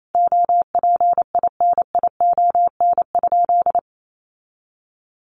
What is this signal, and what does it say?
Morse code, heard as a keyed tone, about 28 wpm, 700 Hz. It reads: OPINION?
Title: Sentences from Top 1200 Words 28wpm